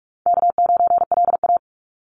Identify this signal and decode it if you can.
Morse code, heard as a keyed tone, about 46 wpm, 700 Hz. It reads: K9LA
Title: Easy Call Signs - 45wpm